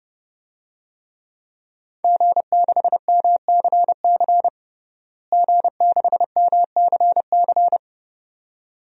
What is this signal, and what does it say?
Morse code, heard as a keyed tone, about 30 wpm, 700 Hz. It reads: G6MCC G6MCC
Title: DXCC-100 Call Signs - 5 Characters - Repeated 2x - 30wpm